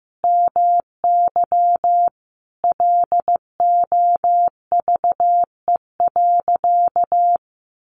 Morse code, heard as a keyed tone, about 15 wpm, 700 Hz. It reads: MY LOVE.
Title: Sentences from Top 1100 Words 15wpm